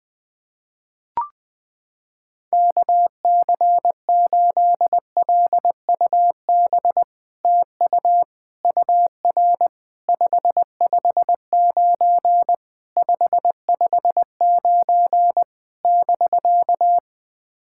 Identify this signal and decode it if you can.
Morse code, heard as a keyed tone, about 20 wpm, 700 Hz. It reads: KC8LUB TU UR 559 559 <BK>
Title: POTA QSO Practice - 20wpm